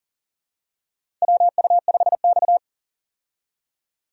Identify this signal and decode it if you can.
Morse code, heard as a keyed tone, about 40 wpm, 700 Hz. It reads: WU5X
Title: USA Call Signs - 4 Characters - 40wpm